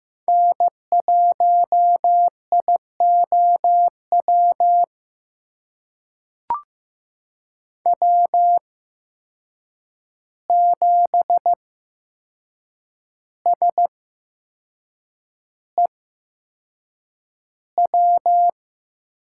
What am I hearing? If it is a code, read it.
Morse code, heard as a keyed tone, about 15 wpm, spaced as for 6 wpm, 700 Hz. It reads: N1IOW W7SEW
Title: USA Call Signs - 5 Characters - 8x Character Spacing - 15wpm